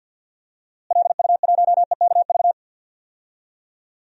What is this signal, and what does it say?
Morse code, heard as a keyed tone, about 52 wpm, 700 Hz. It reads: LU1EKV
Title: DXCC-100 Call Signs - 6 Characters - Repeated 3x - 50wpm